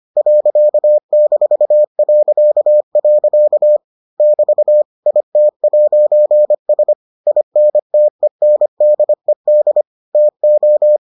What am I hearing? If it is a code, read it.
Morse code, heard as a keyed tone, about 25 wpm, 600 Hz. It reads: .-.. <BT> IT'S INTENDED TO